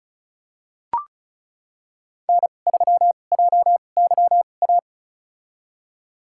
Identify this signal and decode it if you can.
Morse code, heard as a keyed tone, about 34 wpm, spaced as for 25 wpm, 700 Hz. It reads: N3JYA
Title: USA Call Signs - 2x Character Spacing - 35wpm